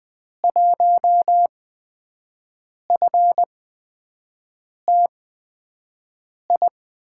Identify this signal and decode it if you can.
Morse code, heard as a keyed tone, about 20 wpm, spaced as for 5 wpm, 700 Hz. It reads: 1FTI
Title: USA Call Signs - 6 Characters - 8x Character Spacing - 20wpm